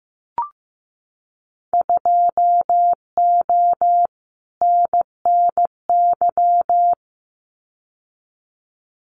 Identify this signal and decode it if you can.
Morse code, heard as a keyed tone, about 15 wpm, 700 Hz. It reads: 2O NNY
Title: Winter Field Day Exchanges 15wpm